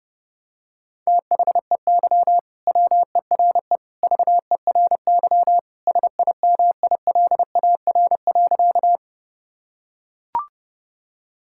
Morse code, heard as a keyed tone, about 30 wpm, 700 Hz. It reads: THEY WERE VERY SIMILAR.
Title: Sentences from Top 1000 Words - Review Part 5 - 30wpm